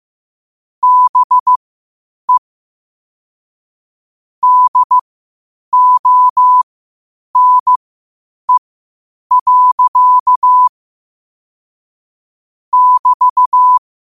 Morse code, heard as a keyed tone, about 15 wpm, spaced as for 8 wpm, 1000 Hz. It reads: BE DONE. <BT>
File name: qotdfiles / 8wpm